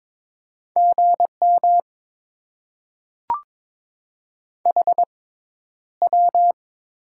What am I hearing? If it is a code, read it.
Morse code, heard as a keyed tone, about 22 wpm, 700 Hz. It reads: G M H W
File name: Top 20 QSO Elements - 6x Character Spacing - 22wpm